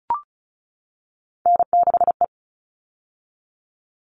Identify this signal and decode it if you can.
Morse code, heard as a keyed tone, about 34 wpm, 700 Hz. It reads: N6E
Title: All Cut Numbers - 3-Digit 35wpm